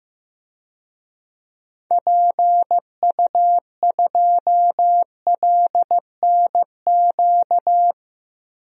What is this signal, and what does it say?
Morse code, heard as a keyed tone, about 15 wpm, 700 Hz. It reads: PU2LNQ
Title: DXCC-100 Call Signs - 6 Characters - Repeated 2x - 15wpm